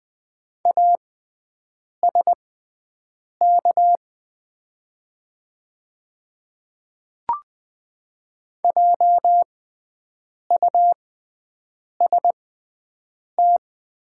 Morse code, heard as a keyed tone, about 20 wpm, spaced as for 7 wpm, 700 Hz. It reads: ASK JUST
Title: 6x Character Spacing - Top 200 Words - 20wpm